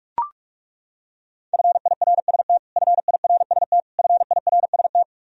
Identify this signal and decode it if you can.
Morse code, heard as a keyed tone, about 41 wpm, 700 Hz. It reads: FIRST FIRST FIRST
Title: Top 100 Words Repeated 45wpm